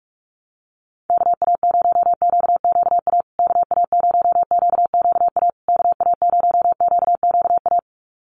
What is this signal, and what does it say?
Morse code, heard as a keyed tone, about 46 wpm, 700 Hz. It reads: KA0QQA KA0QQA KA0QQA
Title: USA Call Signs Repeated 3x 45wpm